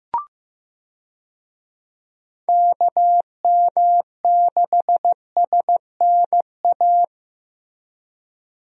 Morse code, heard as a keyed tone, about 15 wpm, 700 Hz. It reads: KM6SNA
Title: USA Call Signs - 6 Characters - Repeated 3x - 15wpm